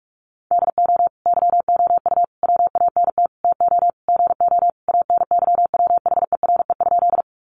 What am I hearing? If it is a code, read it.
Morse code, heard as a keyed tone, about 44 wpm, 700 Hz. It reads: DO YOU WANT TO GO ANYWHERE?
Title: Sentences from Top 1400 Words 45wpm